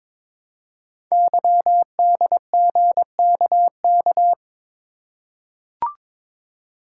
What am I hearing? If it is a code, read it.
Morse code, heard as a keyed tone, about 23 wpm, 700 Hz. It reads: YDGKK
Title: Code Groups - 5 Letters - 22wpm